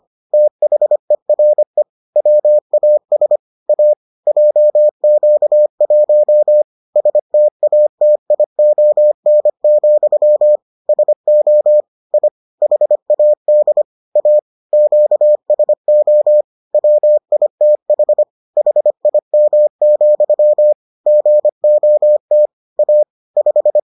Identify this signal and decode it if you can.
Morse code, heard as a keyed tone, about 25 wpm, 600 Hz. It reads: THERE WAS A JQ1 STATION, SO I HAD A QSO WITH HIM, GOT A 5